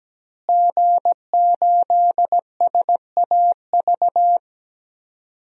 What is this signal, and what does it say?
Morse code, heard as a keyed tone, about 17 wpm, 700 Hz. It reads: G8SAV
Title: DXCC-100 Call Signs - 5 Characters - Repeated 2x - 17wpm